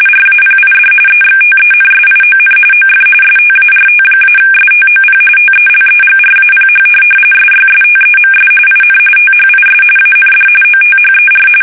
RTTY, about 45.45 baud. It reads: RAPID EXTENSION OF START-STOP TELEGRAPH SYSTEMS IN THIS AND OTHER